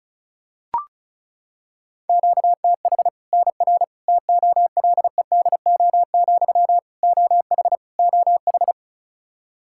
Morse code, heard as a keyed tone, about 36 wpm, 700 Hz. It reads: QTH NR TOLEDO, OH OH
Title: Practice QSO - Exchange 1 - Beginning - 35wpm